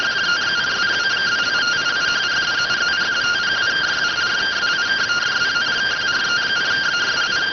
Two-tone teleprinter signal, about 170 Hz shift.